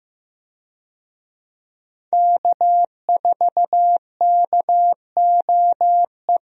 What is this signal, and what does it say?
Morse code, heard as a keyed tone, about 15 wpm, 700 Hz. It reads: K4KOE
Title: USA Call Signs - 5 Characters - Repeated 3x - 15wpm